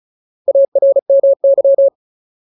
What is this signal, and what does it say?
Morse code, heard as a keyed tone, about 35 wpm, 550 Hz. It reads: ARMY
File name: Ultimate Word List / 35WPM Ultimate